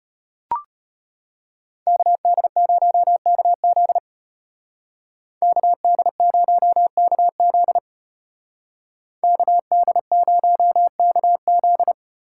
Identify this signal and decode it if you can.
Morse code, heard as a keyed tone, about 33 wpm, 700 Hz. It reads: KD0KZ KD0KZ KD0KZ